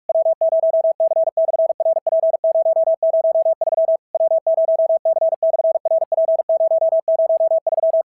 Morse code, heard as a keyed tone, about 44 wpm, 650 Hz. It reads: W0C/RP003 W0C/RP003
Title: SOTA QSO Practice 45wpm